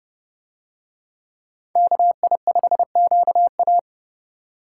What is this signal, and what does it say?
Morse code, heard as a keyed tone, about 27 wpm, 700 Hz. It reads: KI5QA